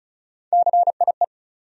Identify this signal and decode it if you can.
Morse code, heard as a keyed tone, about 34 wpm, 700 Hz. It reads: CIE
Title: USA Call Sign Postfixes - 3 Letters 35wpm